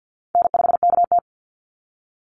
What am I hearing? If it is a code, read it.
Morse code, heard as a keyed tone, about 48 wpm, 700 Hz. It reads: N5KT
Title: Easy Call Signs Repeated 2x - 50wpm